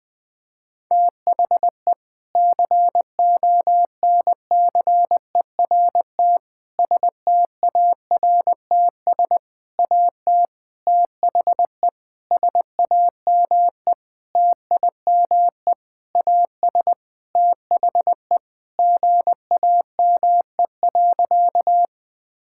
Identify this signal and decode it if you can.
Morse code, heard as a keyed tone, about 19 wpm, 700 Hz. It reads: THE CONCERT STARTS AT THE SAME TIME AS THE GAME.
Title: Sentences with 6-Word Phrases - Spoken Beginning - 20wpm